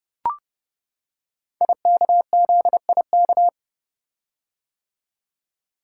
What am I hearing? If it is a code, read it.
Morse code, heard as a keyed tone, about 30 wpm, 700 Hz. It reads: IKZIK
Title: Code Groups - 5 Letters - 30wpm